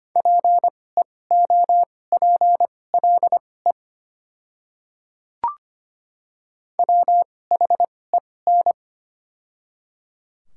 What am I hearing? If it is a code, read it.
Morse code, heard as a keyed tone, about 25 wpm, spaced as for 18 wpm, 700 Hz. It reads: PEOPLE WHEN
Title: 2x Character Spacing - Top 100 Words - 25wpm